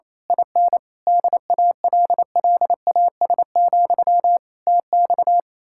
Morse code, heard as a keyed tone, about 26 wpm, 700 Hz. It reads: IN DALLAS, TX
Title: Speed-Racing Common QSO Phrases Encore - 28wpm